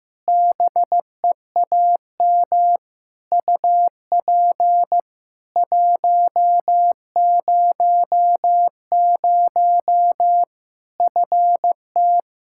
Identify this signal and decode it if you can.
Morse code, heard as a keyed tone, about 15 wpm, 700 Hz. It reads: BEAM UP 100 FT
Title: K1USN SST QSO Practice 15wpm